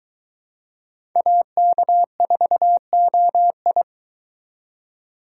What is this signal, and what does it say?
Morse code, heard as a keyed tone, about 22 wpm, 700 Hz. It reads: AK4OI